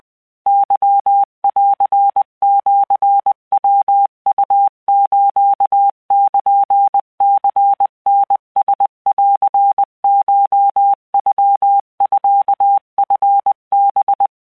Morse code (Japanese Wabun code), heard as a keyed tone, about 20 wpm, 800 Hz. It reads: ケンシヤウスルニタランコノミチハ